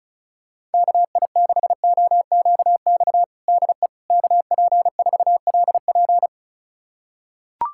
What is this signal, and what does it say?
Morse code, heard as a keyed tone, about 36 wpm, 700 Hz. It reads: KI6OQX DE KP4LP
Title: USA Callsign DE Callsign 35wpm